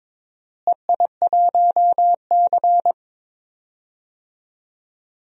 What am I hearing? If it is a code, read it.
Morse code, heard as a keyed tone, about 22 wpm, 700 Hz. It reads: EI1C
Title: DXCC-100 Call Signs - 4 Characters - Repeated 3x - 22wpm